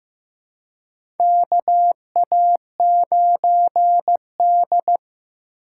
Morse code, heard as a keyed tone, about 15 wpm, 700 Hz. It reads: KA9D
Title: USA Call Signs - 4 Characters - 4x Character Spacing - 15wpm